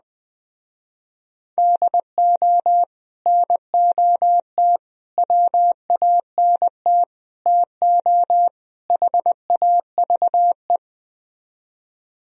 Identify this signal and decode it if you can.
Morse code, heard as a keyed tone, about 20 wpm, 700 Hz. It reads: DO NOT WANT TO HAVE